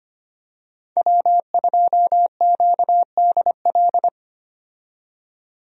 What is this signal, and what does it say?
Morse code, heard as a keyed tone, about 25 wpm, 700 Hz. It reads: W2QDL
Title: USA Call Signs - 5 Characters - 25wpm